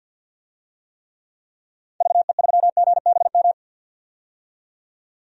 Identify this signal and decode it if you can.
Morse code, heard as a keyed tone, about 49 wpm, 700 Hz. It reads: VE3GBM
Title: DXCC-100 Call Signs - 6 Characters - 50wpm